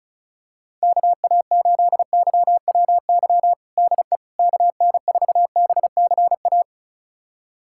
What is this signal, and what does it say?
Morse code, heard as a keyed tone, about 36 wpm, 700 Hz. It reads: KA8YWY DE KN4BCA
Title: USA Callsign DE Callsign - 6 Characters - 35wpm